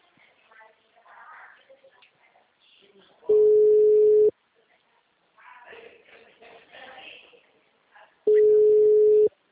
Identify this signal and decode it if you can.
Ringback tone (Europe, 425 Hz), with speech.